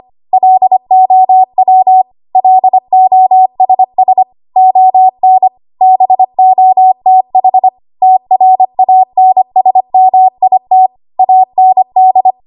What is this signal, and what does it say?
Morse code, heard as a keyed tone, about 25 wpm, 750 Hz. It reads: LOW LOSS ON BOTH TRANSMIT AND
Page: W1AW Web Code Practice Files